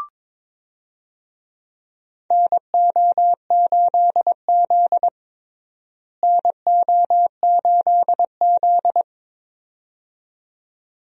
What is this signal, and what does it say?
Morse code, heard as a keyed tone, about 22 wpm, 700 Hz. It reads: NO8Z NO8Z
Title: USA Call Signs - 4 Characters - Repeated 2x - 22wpm